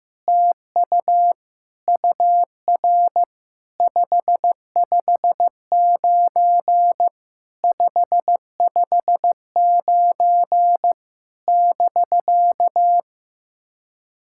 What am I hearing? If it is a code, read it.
Morse code, heard as a keyed tone, about 16 wpm, 700 Hz. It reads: TU UR 559 559 <BK>
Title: SOTA QSO Practice 15wpm